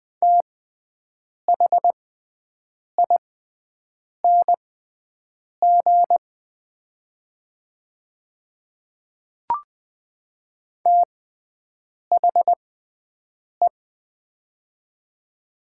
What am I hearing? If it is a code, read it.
Morse code, heard as a keyed tone, about 20 wpm, spaced as for 7 wpm, 700 Hz. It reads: THING THE
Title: 6x Character Spacing - Top 100 Words - 20wpm